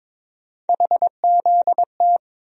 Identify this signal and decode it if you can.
Morse code, heard as a keyed tone, about 22 wpm, 700 Hz. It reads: HZT
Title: USA Call Sign Postfixes - 3 Letters 22wpm